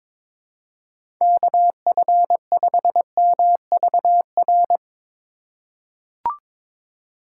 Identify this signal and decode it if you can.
Morse code, heard as a keyed tone, about 22 wpm, 700 Hz. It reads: KF5MVR
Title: USA Call Signs - 3x Character Spacing - 22wpm